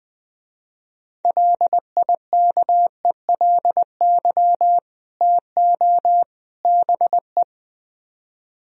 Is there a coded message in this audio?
Yes: Morse code from 1.2 to 7.4 s, heard as a keyed tone, about 20 wpm, 700 Hz. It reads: LIKELY TO BE